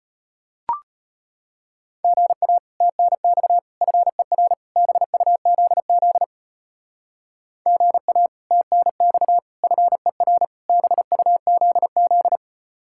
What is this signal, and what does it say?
Morse code, heard as a keyed tone, about 33 wpm, 700 Hz. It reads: GA TNX FER BUZZ GA TNX FER BUZZ